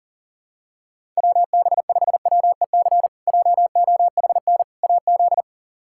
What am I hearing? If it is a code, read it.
Morse code, heard as a keyed tone, about 37 wpm, 700 Hz. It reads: WB5WEC JOHN AZ
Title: K1USN SST QSO Practice - 40wpm